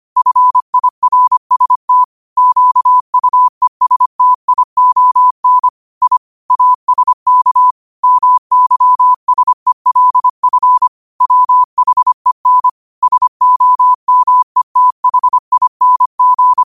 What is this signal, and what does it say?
Morse code, heard as a keyed tone, about 25 wpm, 1000 Hz. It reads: FIRST QUESTION I ASK MYSELF WHEN SOMETHING
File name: qotdfiles / 25wpm